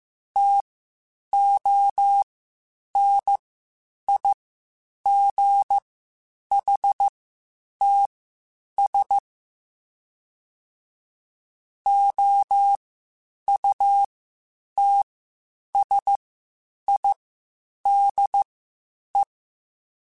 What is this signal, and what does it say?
Morse code, heard as a keyed tone, about 15 wpm, spaced as for 8 wpm, 800 Hz. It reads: TONIGHTS OUTSIDE
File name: Spaced Characters / Spaced Characters-mp3s / Weather5 22 sp @13